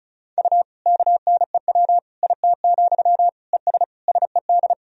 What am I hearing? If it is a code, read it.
Morse code, heard as a keyed tone, about 34 wpm, 700 Hz. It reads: U KNEW IT, ES SED
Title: Sentences from Top 500 Words and Abbreviations 35wpm